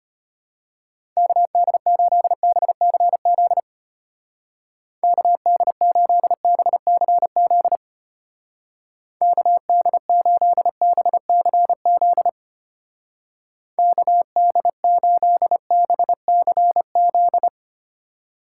Morse code, heard as a keyed tone, about 30 wpm, 700 Hz. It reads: KD8BCZ KD8BCZ KD8BCZ KD8BCZ
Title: Speed Racing - USA Call Signs 25wpm